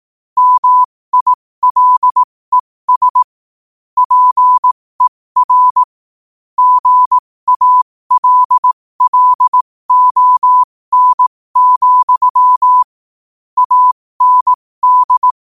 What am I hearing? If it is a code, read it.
Morse code, heard as a keyed tone, about 18 wpm, spaced as for 15 wpm, 1000 Hz. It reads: MILES PER GALLON, AND
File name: qotdfiles / 15wpm